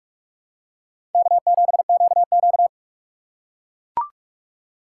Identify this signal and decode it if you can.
Morse code, heard as a keyed tone, about 44 wpm, 700 Hz. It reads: K7QQ
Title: USA Call Signs - 4 Characters - 3x Character Spacing - 45wpm